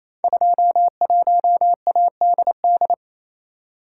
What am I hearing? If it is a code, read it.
Morse code, heard as a keyed tone, about 28 wpm, 700 Hz. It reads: 21ADD
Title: Code Groups - 5 Letter:Numbers - 28wpm